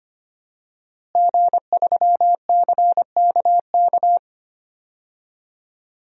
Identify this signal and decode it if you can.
Morse code, heard as a keyed tone, about 25 wpm, 700 Hz. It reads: G3CKK
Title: DXCC-100 Call Signs - 5 Characters - Repeated 2x - 25wpm